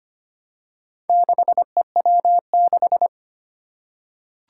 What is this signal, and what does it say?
Morse code, heard as a keyed tone, about 25 wpm, 700 Hz. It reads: 6EW6
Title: Four Challenging Letter-Number Combinations 25wpm